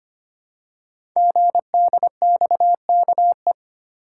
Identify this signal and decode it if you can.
Morse code, heard as a keyed tone, about 25 wpm, 700 Hz. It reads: GDXKE
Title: Code Groups - 5 Letters - 25wpm